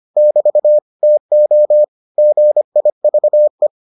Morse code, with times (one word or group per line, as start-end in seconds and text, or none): 0.2-0.8 <BT>
1.0-1.8 TO
2.2-3.7 GIVE